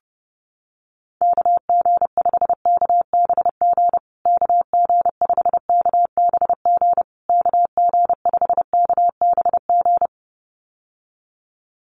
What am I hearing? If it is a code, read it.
Morse code, heard as a keyed tone, about 30 wpm, 700 Hz. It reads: KG5KBG KG5KBG KG5KBG
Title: USA Call Signs Repeated 3x 30wpm